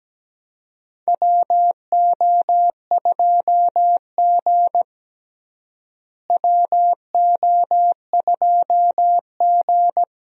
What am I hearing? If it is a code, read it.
Morse code, heard as a keyed tone, about 17 wpm, 700 Hz. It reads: WO2G WO2G
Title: USA Call Signs - 4 Characters - Repeated 2x - 17wpm